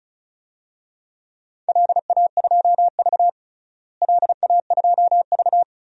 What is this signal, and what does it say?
Morse code, heard as a keyed tone, about 34 wpm, 700 Hz. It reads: LA2V LA2V
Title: DXCC-100 Call Signs - 4 Characters - Repeated 3x - 35wpm